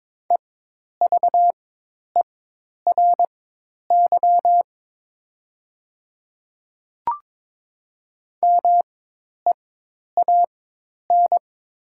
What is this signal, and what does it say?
Morse code, heard as a keyed tone, about 22 wpm, spaced as for 10 wpm, 700 Hz. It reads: EVERY MEAN
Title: 4x Character Spacing - Top 200 Words - 22wpm